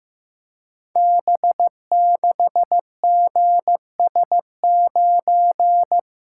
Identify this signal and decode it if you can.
Morse code, heard as a keyed tone, about 15 wpm, 700 Hz. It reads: B6GS9
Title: Code Groups - 5 Letter:Numbers - 15wpm